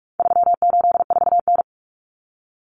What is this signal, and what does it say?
Morse code, heard as a keyed tone, about 44 wpm, 700 Hz. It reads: 384N
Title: Common Cut Numbers - 4-Digit 45wpm